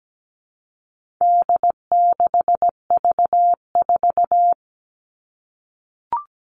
Morse code, heard as a keyed tone, about 17 wpm, 700 Hz. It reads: D6V4
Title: All Cut Numbers - 4-Digit 17wpm